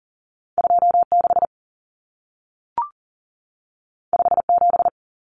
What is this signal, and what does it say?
Morse code, heard as a keyed tone, about 40 wpm, 700 Hz. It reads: 26 57
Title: Common Cut Numbers - 2-Digit 40wpm